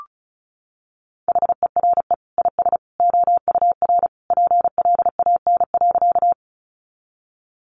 Morse code, heard as a keyed tone, about 34 wpm, 700 Hz. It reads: HERE IS OUR PLAN.
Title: Sentences from Top 500 Words 35wpm